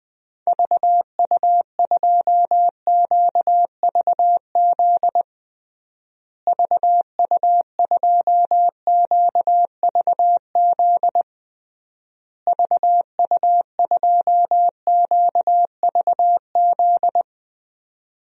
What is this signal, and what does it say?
Morse code, heard as a keyed tone, about 20 wpm, 700 Hz. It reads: VU2QVZ VU2QVZ VU2QVZ
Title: DXCC-100 Call Signs - 6 Characters - Repeated 3x - 20wpm